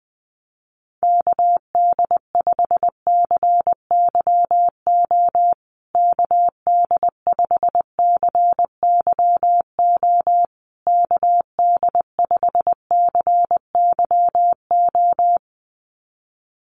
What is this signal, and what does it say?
Morse code, heard as a keyed tone, about 20 wpm, 700 Hz. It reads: KD5CYO KD5CYO KD5CYO
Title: USA Call Signs Repeated 3x 20wpm